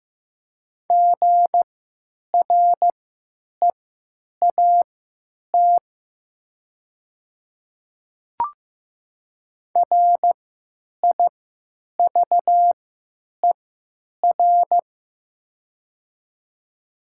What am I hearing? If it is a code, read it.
Morse code, heard as a keyed tone, about 15 wpm, spaced as for 8 wpm, 700 Hz. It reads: GREAT RIVER
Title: 3x Character Spacing - Top 300 Words - 15wpm